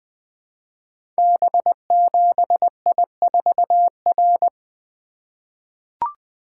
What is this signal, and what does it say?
Morse code, heard as a keyed tone, about 20 wpm, 700 Hz. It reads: B7I4R
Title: Code Groups - 5 Letter:Numbers - 20wpm